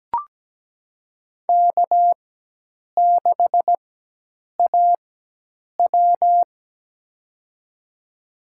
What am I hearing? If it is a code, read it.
Morse code, heard as a keyed tone, about 17 wpm, spaced as for 8 wpm, 700 Hz. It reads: K6AW
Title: USA Call Signs - 4 Characters - 4x Character Spacing - 17wpm